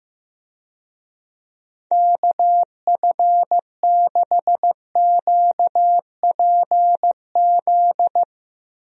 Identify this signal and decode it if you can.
Morse code, heard as a keyed tone, about 16 wpm, 700 Hz. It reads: KF6QPZ
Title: USA Call Signs - 6 Characters - Repeated 3x - 15wpm